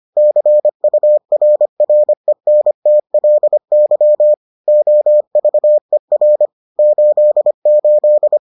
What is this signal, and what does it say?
Morse code, heard as a keyed tone, about 25 wpm, 600 Hz. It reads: CURRENTLY OVER 88